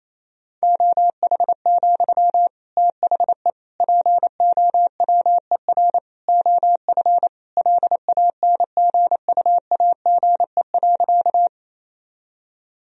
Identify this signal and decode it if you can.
Morse code, heard as a keyed tone, about 28 wpm, 700 Hz. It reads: OH, THE POWER OF LANGUAGE.
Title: Sentences from Top 1000 Words - Review Part 3 - 28wpm